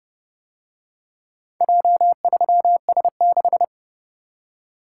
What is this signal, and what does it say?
Morse code, heard as a keyed tone, about 30 wpm, 700 Hz. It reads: J3S6